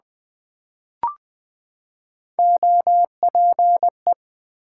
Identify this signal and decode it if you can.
Morse code, heard as a keyed tone, about 20 wpm, 700 Hz. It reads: OPE
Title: Build A Word - 3 Letters At A Time - 9-Letter Words - 20wpm